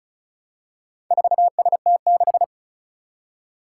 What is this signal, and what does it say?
Morse code, heard as a keyed tone, about 36 wpm, 700 Hz. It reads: 4ST6
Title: Four Challenging Letter-Number Combinations 35wpm